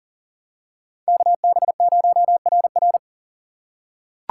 Morse code, heard as a keyed tone, about 40 wpm, 700 Hz. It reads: KB0RR
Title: USA Call Signs - 5 Characters - 4x Character Spacing - 40wpm